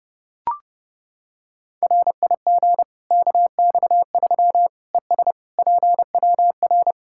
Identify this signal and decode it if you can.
Morse code, heard as a keyed tone, about 32 wpm, 700 Hz. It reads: RIG KX3 ES PWR
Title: Practice QSO - Exchange 2 - Info - 30wpm